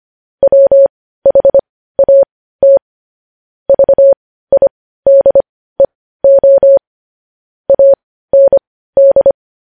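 Morse code, heard as a keyed tone, about 25 wpm, spaced as for 15 wpm, 550 Hz. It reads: WHAT VIDEO AND
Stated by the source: Morse code news-headline practice sent at 15 words per minute.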